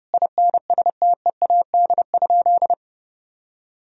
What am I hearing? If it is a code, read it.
Morse code, heard as a keyed tone, about 30 wpm, 700 Hz. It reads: INSTEAD?
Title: Sentences from Top 1000 Words - Review Part 4 - 30wpm